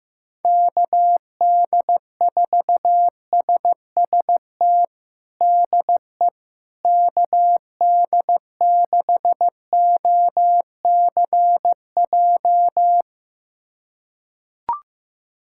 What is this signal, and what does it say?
Morse code, heard as a keyed tone, about 16 wpm, 700 Hz. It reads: KD4SST DE KD6OCJ
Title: USA Callsign DE Callsign - 6 Characters - 15wpm